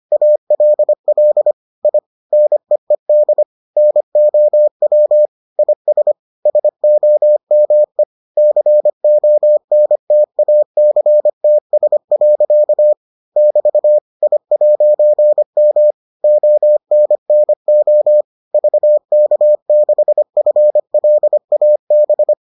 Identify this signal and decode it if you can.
Morse code, heard as a keyed tone, about 25 wpm, 600 Hz. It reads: ALL I NEED NOW IS SOME CONTACTS. <BT> I'M ONNO VK6FLAB